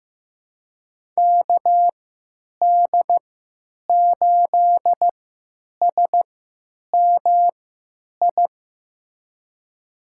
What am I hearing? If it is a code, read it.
Morse code, heard as a keyed tone, about 15 wpm, spaced as for 9 wpm, 700 Hz. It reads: KD8SMI
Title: USA Call Signs - 3x Character Spacing - 15wpm